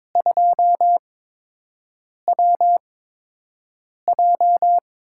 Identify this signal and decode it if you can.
Morse code, heard as a keyed tone, about 22 wpm, spaced as for 6 wpm, 700 Hz. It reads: 2WJ
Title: USA Call Signs - 4 Characters - 8x Character Spacing - 22wpm